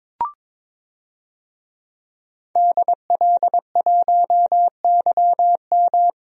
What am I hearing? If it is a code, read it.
Morse code, heard as a keyed tone, about 22 wpm, 700 Hz. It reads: DL1YM
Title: DXCC-100 Call Signs - 5 Characters - Repeated 2x - 22wpm